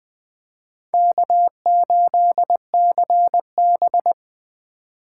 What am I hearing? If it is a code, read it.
Morse code, heard as a keyed tone, about 21 wpm, 700 Hz. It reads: K8CB